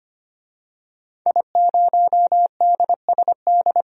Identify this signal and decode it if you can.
Morse code, heard as a keyed tone, about 25 wpm, 700 Hz. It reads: I0DSD
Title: DXCC-100 Call Signs - 5 Characters - 25wpm